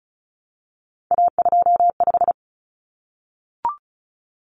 Morse code, heard as a keyed tone, about 34 wpm, 700 Hz. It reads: A25
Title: Common Cut Numbers - 3-Digit 35wpm